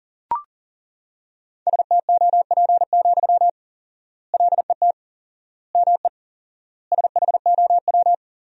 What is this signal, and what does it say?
Morse code, heard as a keyed tone, about 40 wpm, 700 Hz. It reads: STOP, LET ME SHOW
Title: 4x Word-Spacing - Sentences from Top 300 Words - 40wpm